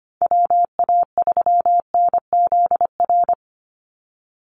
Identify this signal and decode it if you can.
Morse code, heard as a keyed tone, about 25 wpm, 700 Hz. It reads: WA3NZR
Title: Call Signs Repeated 3x - 25wpm